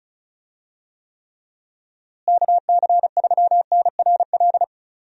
Morse code, heard as a keyed tone, about 36 wpm, 700 Hz. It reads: KC3NRL
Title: USA Call Signs - 6 Characters - Repeated 3x - 35wpm